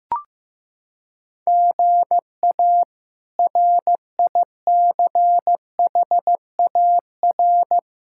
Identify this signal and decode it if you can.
Morse code, heard as a keyed tone, about 15 wpm, 700 Hz. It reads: GA RICHAR
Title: K1USN SST QSO Practice - 15wpm